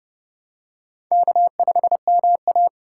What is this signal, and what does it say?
Morse code, heard as a keyed tone, about 29 wpm, 700 Hz. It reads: K5MA
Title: K1USN SST QSO Practice - 30wpm